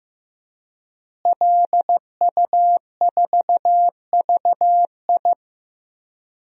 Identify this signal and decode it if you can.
Morse code, heard as a keyed tone, about 16 wpm, 700 Hz. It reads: LU4VI
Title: DXCC-100 Call Signs - 5 Characters - Repeated 2x - 15wpm